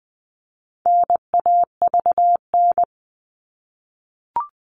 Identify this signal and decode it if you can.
Morse code, heard as a keyed tone, about 20 wpm, 700 Hz. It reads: NAVN
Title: All Cut Numbers - 4-Digit 20wpm